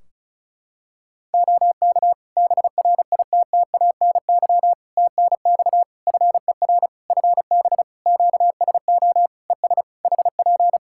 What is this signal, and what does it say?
Morse code, heard as a keyed tone, about 36 wpm, 700 Hz. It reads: OK BRITTANY TNX FER FB QSO ES HP
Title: K1USN SST QSO Practice 35wpm